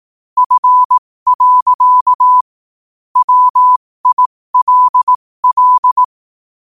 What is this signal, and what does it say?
Morse code, heard as a keyed tone, about 18 wpm, spaced as for 15 wpm, 1000 Hz. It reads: F. WILL
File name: qotdfiles / 15wpm